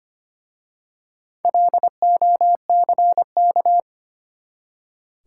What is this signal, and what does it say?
Morse code, heard as a keyed tone, about 25 wpm, 700 Hz. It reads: LOCK
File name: practice / 25wpm / Easy Words with Hints - 25wpm